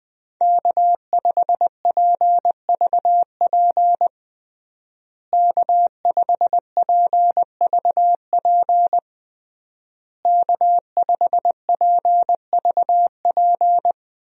Morse code, heard as a keyed tone, about 20 wpm, 700 Hz. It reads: K5PVP K5PVP K5PVP
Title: USA Call Signs - 5 Characters - Repeated 3x - 20wpm